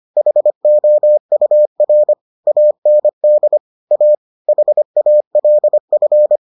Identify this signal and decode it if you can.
Morse code, heard as a keyed tone, about 25 wpm, 600 Hz. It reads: HOUR AND A HALF